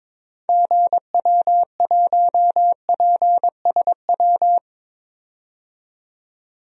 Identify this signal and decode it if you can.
Morse code, heard as a keyed tone, about 21 wpm, 700 Hz. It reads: GW1PSW
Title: DXCC-100 Call Signs - 6 Characters - 22wpm